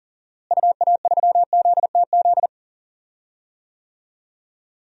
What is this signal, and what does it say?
Morse code, heard as a keyed tone, about 40 wpm, 700 Hz. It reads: UA3ZTZ
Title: DXCC-100 Call Signs - 6 Characters - Repeated 2x - 40wpm